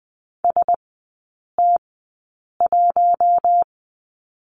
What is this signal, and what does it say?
Morse code, heard as a keyed tone, about 20 wpm, spaced as for 8 wpm, 700 Hz. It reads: ST1
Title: 041 Lesson 8e - Number 1 - Three Characters 20wpm